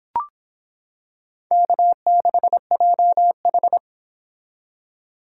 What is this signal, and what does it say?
Morse code, heard as a keyed tone, about 25 wpm, 700 Hz. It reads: K6JH